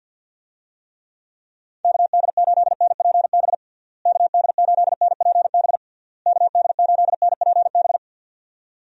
Morse code, heard as a keyed tone, about 46 wpm, 700 Hz. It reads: KD8NPB KD8NPB KD8NPB
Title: USA Call Signs - 6 Characters - Repeated 3x - 50wpm